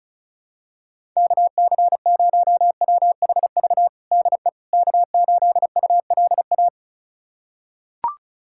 Morse code, heard as a keyed tone, about 34 wpm, 700 Hz. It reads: KC0WHV DE K8ULA
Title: USA Callsign DE Callsign 35wpm